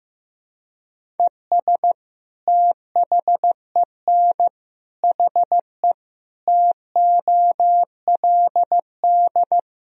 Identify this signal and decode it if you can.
Morse code, heard as a keyed tone, about 15 wpm, 700 Hz. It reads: ES THEN HE TOLD
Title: Sentences from Top 400 Words and Abbreviations 15wpm